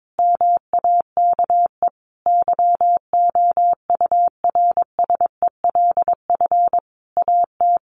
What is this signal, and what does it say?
Morse code, heard as a keyed tone, about 22 wpm, 700 Hz. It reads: MAKE YOURSELF AT
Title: Sentences from Top 600 Words 22wpm